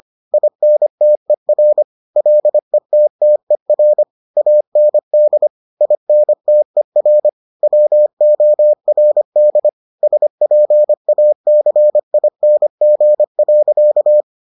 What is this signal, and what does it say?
Morse code, heard as a keyed tone, about 25 wpm, 600 Hz. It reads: INTER LETTER AND INTER WORD SPACING.